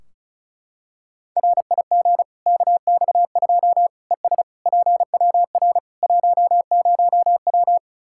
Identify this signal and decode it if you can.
Morse code, heard as a keyed tone, about 34 wpm, 700 Hz. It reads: RIG KX2 ES PWR 10W
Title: K1USN SST QSO Practice 35wpm